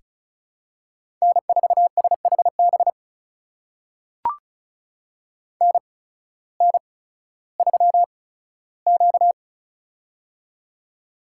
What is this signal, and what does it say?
Morse code, heard as a keyed tone, about 36 wpm, spaced as for 16 wpm, 700 Hz. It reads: N4SHB NN3Q
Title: USA Call Signs - 8x Character Spacing - 35wpm